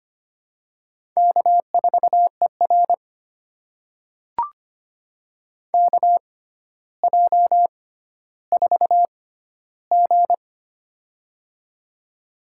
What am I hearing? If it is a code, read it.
Morse code, heard as a keyed tone, about 25 wpm, spaced as for 13 wpm, 700 Hz. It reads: K4ER KJ4G
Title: USA Call Signs - 4 Characters - 6x Character Spacing - 25wpm